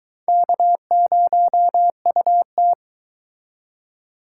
Morse code, heard as a keyed tone, about 22 wpm, 700 Hz. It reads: K0UT